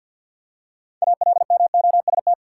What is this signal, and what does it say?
Morse code, heard as a keyed tone, about 52 wpm, 700 Hz. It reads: ALMOST
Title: Six Letter Words - 50wpm